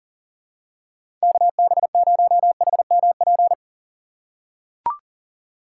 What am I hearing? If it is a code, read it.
Morse code, heard as a keyed tone, about 40 wpm, 700 Hz. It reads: KB0HMP
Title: USA Call Signs - 6x Character Spacing - 40wpm